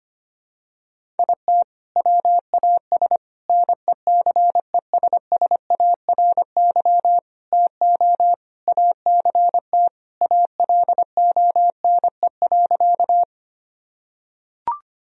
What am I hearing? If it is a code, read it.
Morse code, heard as a keyed tone, about 25 wpm, 700 Hz. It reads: IT WAS NECESSARY TO ACT ALONE.
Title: Sentences from Top 1000 Words - Review Part 1 - 25wpm